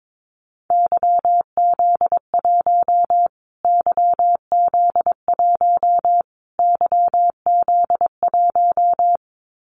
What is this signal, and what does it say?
Morse code, heard as a keyed tone, about 22 wpm, 700 Hz. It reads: YZ1 YZ1 YZ1
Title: Callsign Prefixes for Top 25 DXCC - Repeated 3x - 22wpm